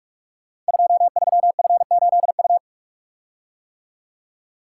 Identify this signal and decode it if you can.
Morse code, heard as a keyed tone, about 44 wpm, 700 Hz. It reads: 23F8U
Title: Code Groups - 5 Letter:Numbers - 45wpm